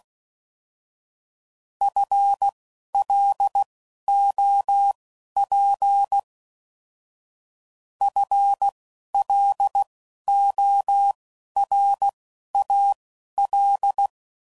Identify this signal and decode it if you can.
Morse code, heard as a keyed tone, about 16 wpm, spaced as for 11 wpm, 800 Hz. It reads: FLOP FLORAL